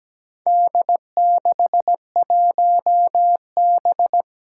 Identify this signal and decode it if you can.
Morse code, heard as a keyed tone, about 17 wpm, 700 Hz. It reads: D61B
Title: Four Challenging Letter-Number Combinations 17wpm